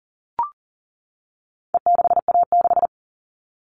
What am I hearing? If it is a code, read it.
Morse code, heard as a keyed tone, about 40 wpm, 700 Hz. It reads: E6A6
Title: All Cut Numbers - 4-Digit 40wpm